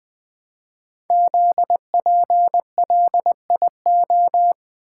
Morse code, heard as a keyed tone, about 20 wpm, 700 Hz. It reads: ZPLIO
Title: Code Groups - 5 Letters - 20wpm